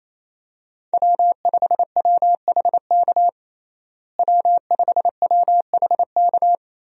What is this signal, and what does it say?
Morse code, heard as a keyed tone, about 28 wpm, 700 Hz. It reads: W5WHK W5WHK
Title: USA Call Signs - 5 Characters - Repeated 3x - 28wpm